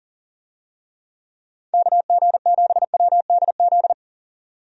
Morse code, heard as a keyed tone, about 40 wpm, 700 Hz. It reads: KG7WDZ
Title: USA Call Signs - 6 Characters - 40wpm